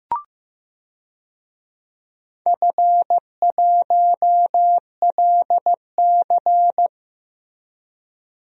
Morse code, heard as a keyed tone, about 15 wpm, 700 Hz. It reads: F1LC
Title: DXCC-100 Call Signs - 4 Characters - 15wpm